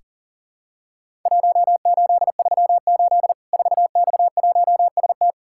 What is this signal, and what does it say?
Morse code, heard as a keyed tone, about 40 wpm, 700 Hz. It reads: 1838 4X1ST
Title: Worked All Europe DX Contest - QTC Exchange - 40wpm